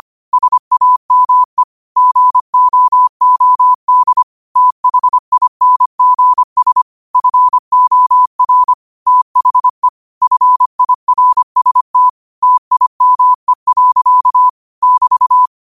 Morse code, heard as a keyed tone, about 25 wpm, 1000 Hz. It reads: SAME GOOD THINGS FOR THE FIRST TIME. <BT>